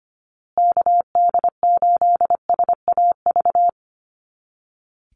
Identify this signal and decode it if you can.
Morse code, heard as a keyed tone, about 25 wpm, 700 Hz. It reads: KD8SAV
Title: Call Signs Repeated 2x - 25wpm